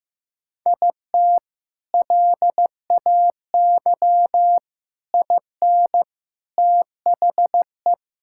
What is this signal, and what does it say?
Morse code, heard as a keyed tone, about 15 wpm, 700 Hz. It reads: IT LAY IN THE
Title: Sentences from Top 1000 Words - Review Part 6 - 15wpm